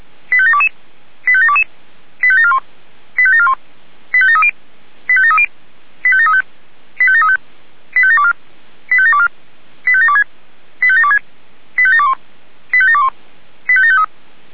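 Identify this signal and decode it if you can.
Five-tone selective call (ZVEI), heard as quick runs of five tones, tones of about 70 ms. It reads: 85722 85722 85741 85741 85730 85730 85735 85735 85724 85724 85726 85726 85721 85721 85753
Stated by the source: SDR recording of a ZVEI five-tone selective call, address 85825.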